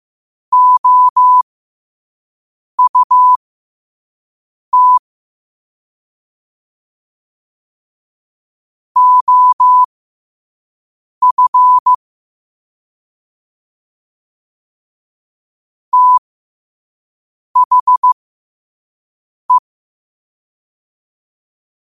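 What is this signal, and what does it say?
Morse code, heard as a keyed tone, about 15 wpm, spaced as for 5 wpm, 1000 Hz. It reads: OUT OF THE
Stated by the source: Morse code practice at 5 wpm.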